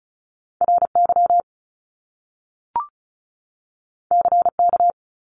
Morse code, heard as a keyed tone, about 36 wpm, 700 Hz. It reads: RY CK
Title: Top Two Letter Combinations in Words 35wpm